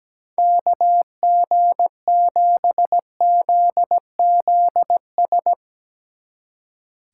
Morse code, heard as a keyed tone, about 17 wpm, 700 Hz. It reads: KG7ZZS
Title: Speed Racing - USA 2x3 Call Signs 17wpm